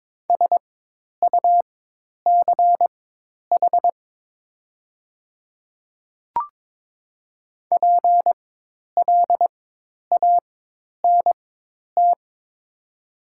Morse code, heard as a keyed tone, about 22 wpm, spaced as for 10 wpm, 690 Hz. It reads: SUCH PLANT
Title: 4x Character Spacing - Top 500 Words - 22wpm